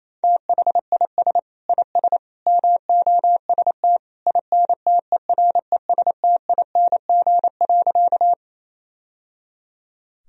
Morse code, heard as a keyed tone, about 28 wpm, 700 Hz. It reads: THIS IS MOST INTERESTING.
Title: Sentences from Top 1000 Words - Review Part 1 - 28wpm